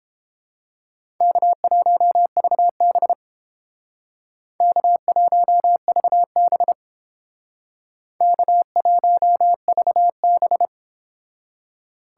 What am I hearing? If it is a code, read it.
Morse code, heard as a keyed tone, about 30 wpm, 700 Hz. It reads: K1VB K1VB K1VB